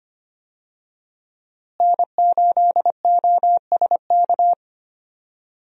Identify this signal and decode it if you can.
Morse code, heard as a keyed tone, about 25 wpm, 700 Hz. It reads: N8OSK
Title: USA Call Signs - 5 Characters - Repeated 2x - 25wpm